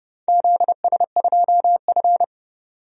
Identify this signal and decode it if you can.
Morse code, heard as a keyed tone, about 30 wpm, 700 Hz. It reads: ZS2F
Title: DXCC-100 Call Signs - 4 Characters - 30wpm